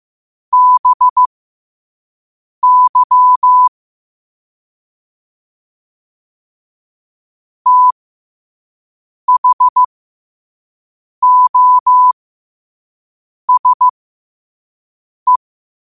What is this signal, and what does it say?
Morse code, heard as a keyed tone, about 15 wpm, spaced as for 5 wpm, 1000 Hz. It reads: BY THOSE
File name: qotdfiles / 5wpm